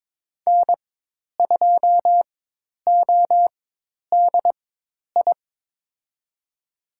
Morse code, heard as a keyed tone, about 22 wpm, spaced as for 10 wpm, 700 Hz. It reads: N2ODI
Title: USA Call Signs - 5 Characters - 4x Character Spacing - 22wpm